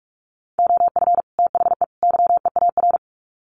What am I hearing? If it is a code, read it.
Morse code, heard as a keyed tone, about 44 wpm, 700 Hz. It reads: OF THE YEAR
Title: Sets of 3 Words 45wpm